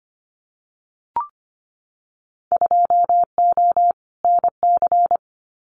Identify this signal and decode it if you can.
Morse code, heard as a keyed tone, about 25 wpm, 700 Hz. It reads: 2O NC
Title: Winter Field Day Exchanges 25wpm